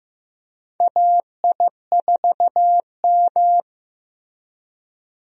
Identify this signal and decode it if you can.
Morse code, heard as a keyed tone, about 16 wpm, 700 Hz. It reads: AI4M
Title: USA Call Signs - 4 Characters - Repeated 3x - 15wpm